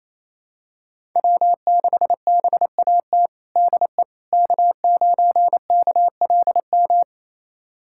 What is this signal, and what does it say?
Morse code, heard as a keyed tone, about 28 wpm, 690 Hz. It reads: W6BAT DE K9KLM
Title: USA Callsign DE Callsign - 5 Characters - 28wpm